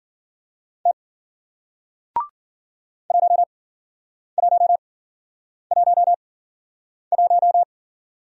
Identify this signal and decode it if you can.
Morse code, heard as a keyed tone, about 53 wpm, spaced as for 9 wpm, 700 Hz. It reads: T 1111